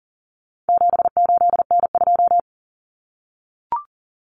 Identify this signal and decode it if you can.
Morse code, heard as a keyed tone, about 40 wpm, 700 Hz. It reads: 78N2
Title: Common Cut Numbers - 4-Digit 40wpm